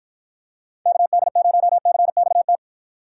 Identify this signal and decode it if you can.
Morse code, heard as a keyed tone, about 51 wpm, 700 Hz. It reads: KD0CXT